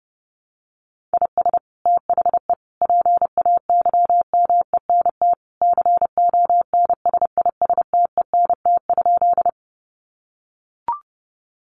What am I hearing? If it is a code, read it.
Morse code, heard as a keyed tone, about 30 wpm, 700 Hz. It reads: IS THE PAYMENT CONSISTENT?
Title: Sentences from Top 1500 Words 30wpm